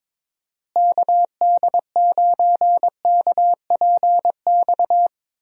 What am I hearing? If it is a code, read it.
Morse code, heard as a keyed tone, about 21 wpm, 700 Hz. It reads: KD9KPX
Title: USA Call Signs - 6 Characters - Repeated 3x - 22wpm